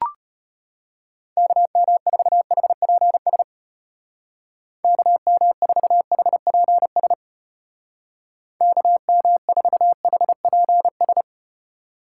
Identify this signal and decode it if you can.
Morse code, heard as a keyed tone, about 34 wpm, 700 Hz. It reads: KM4HPS KM4HPS KM4HPS